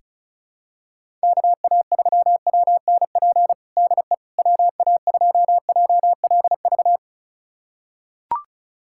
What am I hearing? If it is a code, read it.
Morse code, heard as a keyed tone, about 36 wpm, 700 Hz. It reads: KA3WNP DE WA2JLV
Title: USA Callsign DE Callsign - 6 Characters - 35wpm